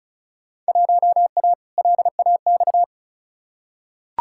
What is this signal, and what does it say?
Morse code, heard as a keyed tone, about 36 wpm, 700 Hz. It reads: 1A LAX
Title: Field Day Exchanges - 35wpm